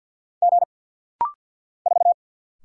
Morse code, heard as a keyed tone, about 50 wpm, 690 Hz. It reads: G 4
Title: Single Letter-Number Alt - Rapid-Fire 50wpm